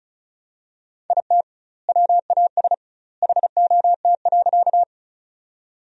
Morse code, heard as a keyed tone, about 36 wpm, 700 Hz. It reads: IT WAS HOT.
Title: 2x Word-Spacing - Sentences from Top 500 Words - 35wpm